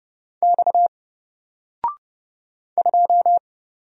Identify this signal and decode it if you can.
Morse code, heard as a keyed tone, about 30 wpm, 700 Hz. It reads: X 2
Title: Single Letter-Number Alt 30wpm